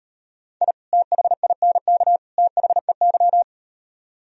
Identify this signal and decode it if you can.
Morse code, heard as a keyed tone, about 38 wpm, 700 Hz. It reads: I THINK THEY